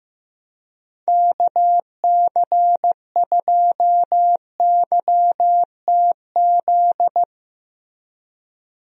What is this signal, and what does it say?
Morse code, heard as a keyed tone, about 15 wpm, 700 Hz. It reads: KC2YTZ
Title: USA Call Signs - 6 Characters - Repeated 2x - 15wpm